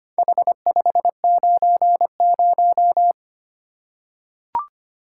Morse code, heard as a keyed tone, about 25 wpm, 700 Hz. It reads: H590
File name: practice / 25wpm / Four Letter-Number Combinations - 25wpm